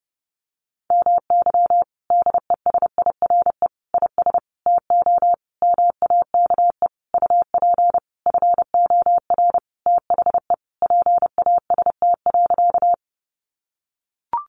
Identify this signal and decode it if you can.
Morse code, heard as a keyed tone, about 30 wpm, 700 Hz. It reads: MY DESIRE IS TO MAKE UP FOR THE PAST.
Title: Sentences from Top 1100 Words 30wpm